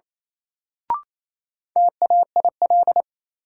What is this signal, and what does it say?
Morse code, heard as a keyed tone, about 28 wpm, 700 Hz. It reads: TAIL
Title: Sentences from Top 1000 Words - Review Part 3 - New Word Drill - 28wpm